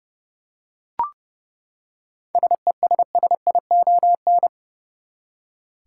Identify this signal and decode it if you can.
Morse code, heard as a keyed tone, about 30 wpm, 700 Hz. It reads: SESSION
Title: Seven Letter Words - 30wpm